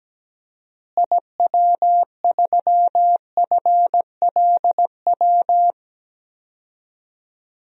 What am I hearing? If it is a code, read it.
Morse code, heard as a keyed tone, about 17 wpm, 700 Hz. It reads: IW3FLW
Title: DXCC-100 Call Signs - 6 Characters - Repeated 3x - 17wpm